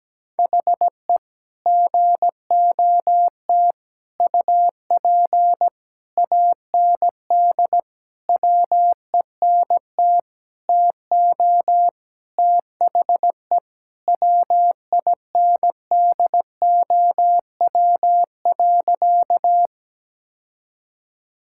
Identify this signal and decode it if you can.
Morse code, heard as a keyed tone, about 17 wpm, 700 Hz. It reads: HE GOT UP AND WENT TO THE WINDOW.
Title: Sentences from Top 1600 Words 17wpm